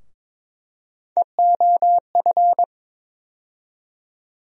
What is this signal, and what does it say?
Morse code, heard as a keyed tone, about 22 wpm, 700 Hz. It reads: EOF
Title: USA Call Sign Postfixes - 3 Letters 22wpm